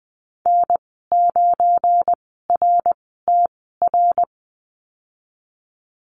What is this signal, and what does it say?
Morse code, heard as a keyed tone, about 19 wpm, spaced as for 14 wpm, 700 Hz. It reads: N9RTR
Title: 173 Lesson 28g - Number 9 - Call Signs 20wpm